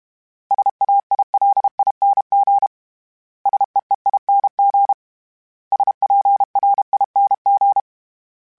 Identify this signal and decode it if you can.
Morse code, heard as a keyed tone, about 32 wpm, 800 Hz. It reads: SAILING SEEING SPRING
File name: Special INGs-mp3s / Special INGs-mp3s / Sngl Ing @30